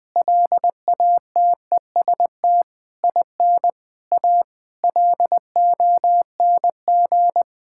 Morse code, heard as a keyed tone, about 19 wpm, 700 Hz. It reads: LATEST IN A LONG
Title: Sentences with 7-Word Phrases - Spoken Beginning - 20wpm